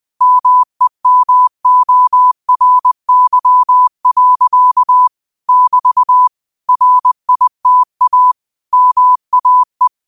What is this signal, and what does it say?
Morse code, heard as a keyed tone, about 20 wpm, 1000 Hz. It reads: MEMORY. <BT> RITA MAE
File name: qotdfiles / 20wpm